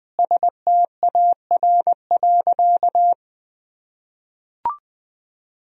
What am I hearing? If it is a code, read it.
Morse code, heard as a keyed tone, about 20 wpm, 700 Hz. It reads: STAR.
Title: Sentences from Top 1000 Words - Review Part 2 - 20wpm